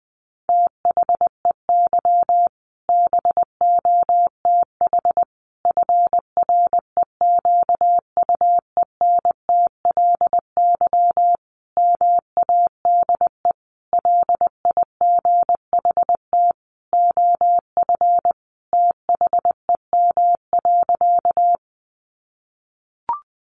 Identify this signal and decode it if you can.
Morse code, heard as a keyed tone, about 19 wpm, 700 Hz. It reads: THEY BOTH FREQUENTLY MADE LIGHT OF THEM.
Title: Sentences from Top 1300 Words 20wpm